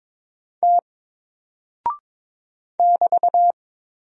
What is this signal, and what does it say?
Morse code, heard as a keyed tone, about 22 wpm, 700 Hz. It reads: T <BT>
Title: Single Character (Letter:Number:Punctuation:Prosign) 22wpm